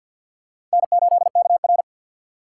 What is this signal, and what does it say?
Morse code, heard as a keyed tone, about 50 wpm, 700 Hz. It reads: N8KR
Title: USA Call Signs - 4x Character Spacing - 50wpm